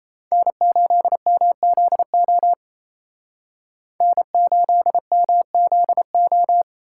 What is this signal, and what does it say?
Morse code, heard as a keyed tone, about 36 wpm, 700 Hz. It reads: N8MZO N8MZO